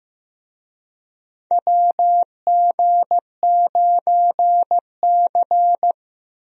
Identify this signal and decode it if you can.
Morse code, heard as a keyed tone, about 15 wpm, 700 Hz. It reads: WG9C
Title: USA Call Signs - 4 Characters - 15wpm